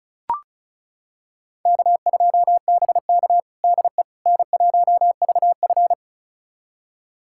Morse code, heard as a keyed tone, about 34 wpm, 700 Hz. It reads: K2BK DE N1VF
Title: USA Callsign DE Callsign - 4 Characters - 35wpm